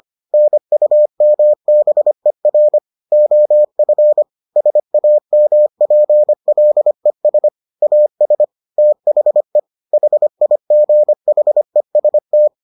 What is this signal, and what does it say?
Morse code, heard as a keyed tone, about 25 wpm, 600 Hz. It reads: NUMBER OF SAMPLES AS THE HIGHEST